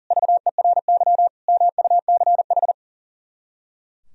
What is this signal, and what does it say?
Morse code, heard as a keyed tone, about 37 wpm, 700 Hz. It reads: VERY MUCH
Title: Speed-Racing - Sets of 4 Words 40wpm